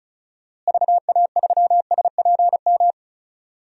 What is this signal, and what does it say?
Morse code, heard as a keyed tone, about 36 wpm, 700 Hz. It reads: VA3SPM
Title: DXCC-100 Call Signs - 6 Characters - Repeated 3x - 35wpm